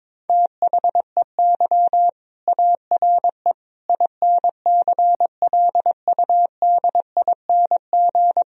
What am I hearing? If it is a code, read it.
Morse code, heard as a keyed tone, about 22 wpm, 700 Hz. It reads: THEY ARE INCLUDING
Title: Sentences from Top 1000 Words - Review Part 4 - 22wpm